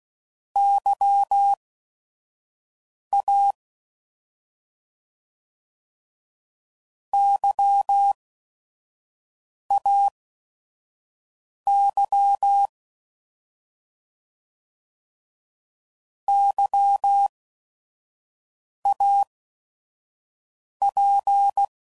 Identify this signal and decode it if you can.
Morse code, heard as a keyed tone, about 16 wpm, spaced as for 5 wpm, 800 Hz. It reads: YA YAY YAP
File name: Prefixes / By First Letter / Ys